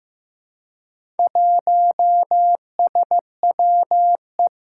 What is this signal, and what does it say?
Morse code, heard as a keyed tone, about 15 wpm, 700 Hz. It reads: 1SWE
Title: Four Challenging Letter-Number Combinations 15wpm